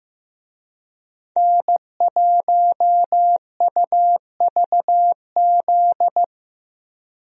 Morse code, heard as a keyed tone, about 15 wpm, 700 Hz. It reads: N1UVZ
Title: USA Call Signs - 5 Characters - 15wpm